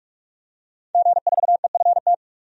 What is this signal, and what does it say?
Morse code, heard as a keyed tone, about 46 wpm, 700 Hz. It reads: G4EFT
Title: DXCC-100 Call Signs - 5 Characters - 45wpm